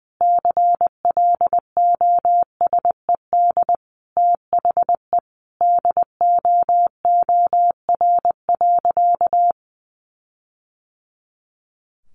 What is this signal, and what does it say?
Morse code, heard as a keyed tone, about 20 wpm, 700 Hz. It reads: CLOSED THE DOOR.
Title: Sentences from Top 1400 Words 20wpm